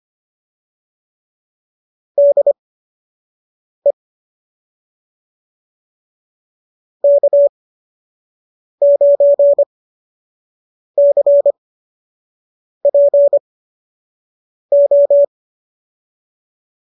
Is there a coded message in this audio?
Yes: Morse code from 2.2 to 15.2 s, heard as a keyed tone, about 25 wpm, spaced as for 6 wpm, 570 Hz. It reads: DE K9CPO